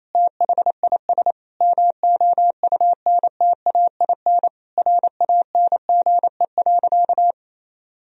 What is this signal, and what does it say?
Morse code, heard as a keyed tone, about 28 wpm, 700 Hz. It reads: THIS MOUNTAIN RANGE.
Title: Sentences from Top 1000 Words - Review Part 4 - 28wpm